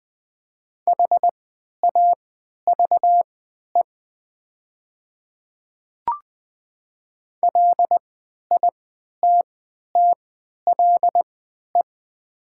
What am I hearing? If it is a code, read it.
Morse code, heard as a keyed tone, about 20 wpm, spaced as for 11 wpm, 700 Hz. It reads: HAVE LITTLE
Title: 3x Character Spacing - Top 300 Words - 20wpm